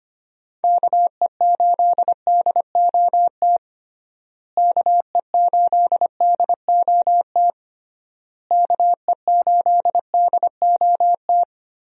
Morse code, heard as a keyed tone, about 25 wpm, 700 Hz. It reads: KE8DOT KE8DOT KE8DOT
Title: USA Call Signs - 6 Characters - Repeated 3x - 25wpm